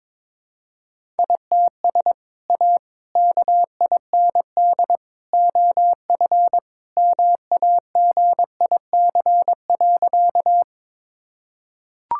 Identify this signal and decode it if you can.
Morse code, heard as a keyed tone, about 22 wpm, 700 Hz. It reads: ITS A KIND OF MAGIC.
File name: practice / 22wpm / Sentences with 3-Word Phrases - 22wpm